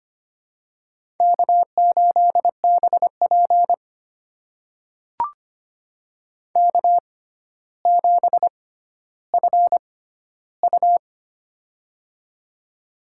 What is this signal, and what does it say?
Morse code, heard as a keyed tone, about 25 wpm, spaced as for 13 wpm, 700 Hz. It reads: K8BP K7FU
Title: USA Call Signs - 4 Characters - 6x Character Spacing - 25wpm